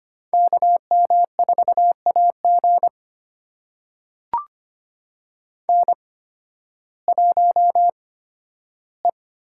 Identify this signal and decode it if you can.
Morse code, heard as a keyed tone, about 25 wpm, spaced as for 13 wpm, 700 Hz. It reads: KM4AG N1E
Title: USA Call Signs - 8x Character Spacing - 25wpm